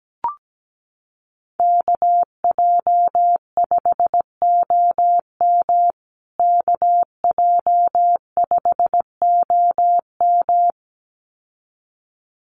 Morse code, heard as a keyed tone, about 17 wpm, 700 Hz. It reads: KJ5OM KJ5OM
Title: USA Call Signs Repeated 2x 17wpm